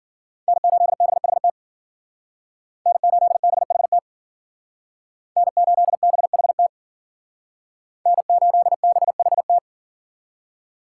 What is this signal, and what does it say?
Morse code, heard as a keyed tone, about 51 wpm, 700 Hz. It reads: N8BHT N8BHT N8BHT N8BHT